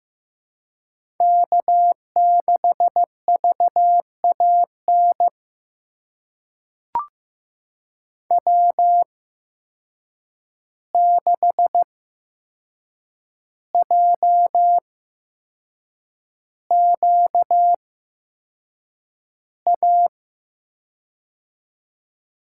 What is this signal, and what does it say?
Morse code, heard as a keyed tone, about 15 wpm, spaced as for 6 wpm, 700 Hz. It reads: K6VAN W6JQA
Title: USA Call Signs - 5 Characters - 8x Character Spacing - 15wpm